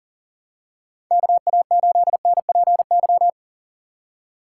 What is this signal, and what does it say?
Morse code, heard as a keyed tone, about 39 wpm, 700 Hz. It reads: KA8NPY
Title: USA Call Signs - 4x Character Spacing - 40wpm